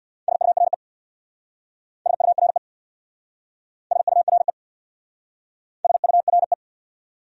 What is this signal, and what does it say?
Morse code, heard as a keyed tone, about 65 wpm, 700 Hz. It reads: SURE SURE SURE SURE